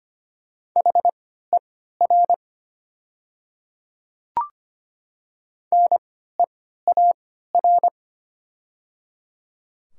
Morse code, heard as a keyed tone, about 25 wpm, spaced as for 14 wpm, 700 Hz. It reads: HER NEAR
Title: 3x Character Spacing - Top 200 Words - 25wpm